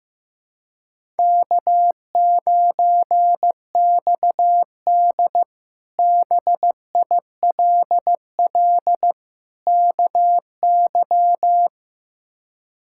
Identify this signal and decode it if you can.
Morse code, heard as a keyed tone, about 15 wpm, 700 Hz. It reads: K9XD BILL KY
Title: K1USN SST QSO Practice - 15wpm